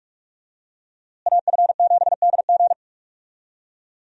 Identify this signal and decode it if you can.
Morse code, heard as a keyed tone, about 46 wpm, 700 Hz. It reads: AF7DG
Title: USA Call Signs - 5 Characters - Repeated 2x - 45wpm